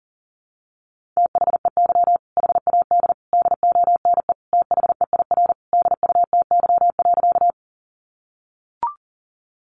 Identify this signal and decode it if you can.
Morse code, heard as a keyed tone, about 40 wpm, 700 Hz. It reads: THEY HAD DONE THEIR DUTY.
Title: Sentences from Top 1500 Words 40wpm